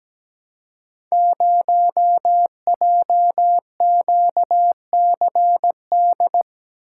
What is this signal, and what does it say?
Morse code, heard as a keyed tone, about 17 wpm, 700 Hz. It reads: 0JQCD
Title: Code Groups - 5 Letter:Numbers - 17wpm